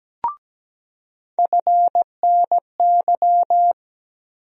Morse code, heard as a keyed tone, about 17 wpm, 700 Hz. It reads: FNY
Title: USA Call Sign Postfixes - 3 Letters 17wpm